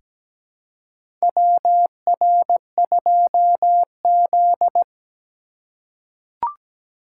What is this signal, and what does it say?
Morse code, heard as a keyed tone, about 17 wpm, 700 Hz. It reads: WR2Z
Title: USA Call Signs - 2x Character Spacing - 17wpm